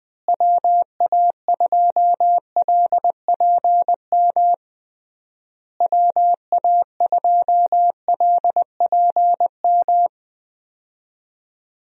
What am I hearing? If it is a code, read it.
Morse code, heard as a keyed tone, about 21 wpm, 700 Hz. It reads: WA2LPM WA2LPM
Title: USA Call Signs - 6 Characters - Repeated 3x - 20wpm